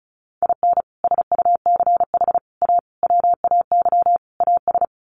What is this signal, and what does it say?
Morse code, heard as a keyed tone, about 34 wpm, 700 Hz. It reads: IN SUCH A WAY AS
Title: Sets of 5 Words Encore 35wpm